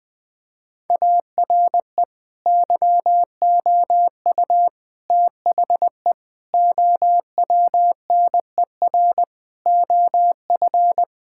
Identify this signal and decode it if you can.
Morse code, heard as a keyed tone, about 20 wpm, 700 Hz. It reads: ARE YOU THE OWNER OF
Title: Sentences from Top 1600 Words 20wpm